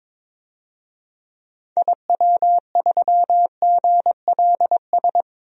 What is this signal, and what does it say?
Morse code, heard as a keyed tone, about 22 wpm, 700 Hz. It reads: IW3GLS
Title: DXCC-100 Call Signs - 6 Characters - 22wpm